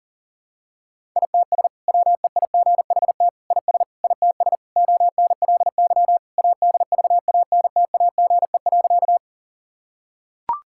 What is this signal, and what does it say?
Morse code, heard as a keyed tone, about 37 wpm, 700 Hz. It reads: ITS WEIGHT IS ITS ONLY ADVANTAGE.
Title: Sentences from Top 1000 Words - Review Part 6 - 40wpm